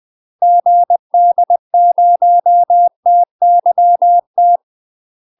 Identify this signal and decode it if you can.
Morse code, heard as a keyed tone, about 20 wpm, 700 Hz. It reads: GD0TYT
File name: UK MASTER QSOs TRAINING / 20WPM / UK MASTER QSOs 0011